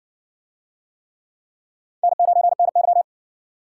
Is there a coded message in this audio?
Yes: Morse code from 2.0 to 3.0 s, heard as a keyed tone, about 54 wpm, 700 Hz. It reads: N9NY